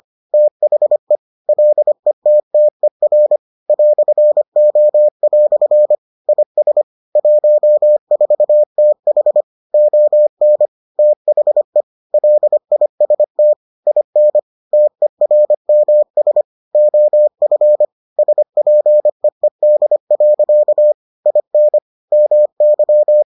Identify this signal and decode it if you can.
Morse code, heard as a keyed tone, about 25 wpm, 600 Hz. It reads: THE LETTER "O" IS 14TH ON THE LIST IN TERMS OF SPEED. IN MY